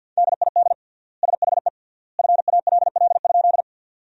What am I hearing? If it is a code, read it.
Morse code, heard as a keyed tone, about 52 wpm, 700 Hz. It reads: DID SHE FALL?
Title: 3x Word-Spacing - Sentences from Top 500 Words - 50wpm